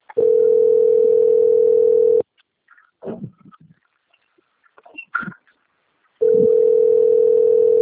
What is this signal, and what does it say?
Ringback tone (North America), 440 + 480 Hz.
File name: out- - - - - .16002